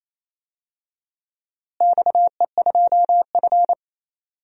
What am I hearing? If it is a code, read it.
Morse code, heard as a keyed tone, about 28 wpm, 700 Hz. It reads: XE2F
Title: DXCC-100 Call Signs - 4 Characters - Repeated 3x - 28wpm